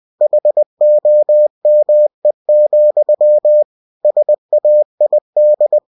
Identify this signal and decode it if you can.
Morse code, heard as a keyed tone, about 20 wpm, 600 Hz. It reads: HOME, SAID
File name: cw-books / winnie-the-pooh / winnie-20wpm-page-0066